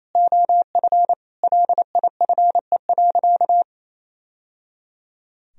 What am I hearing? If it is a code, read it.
Morse code, heard as a keyed tone, about 28 wpm, 700 Hz. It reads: OF LIFE.
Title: Sentences from Top 1000 Words - Review Part 3 - 28wpm